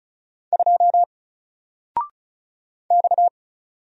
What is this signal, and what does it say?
Morse code, heard as a keyed tone, about 35 wpm, 700 Hz. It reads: 2 X
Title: Single Character (Letter:Number:Punctuation:Prosign) 35wpm